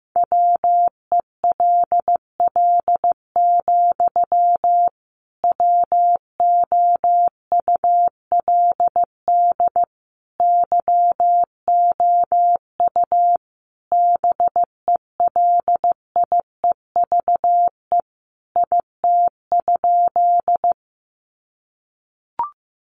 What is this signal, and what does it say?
Morse code, heard as a keyed tone, about 15 wpm, 700 Hz. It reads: WELL, WOULD YOU BELIEVE IT?
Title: Sentences from Top 1000 Words 15wpm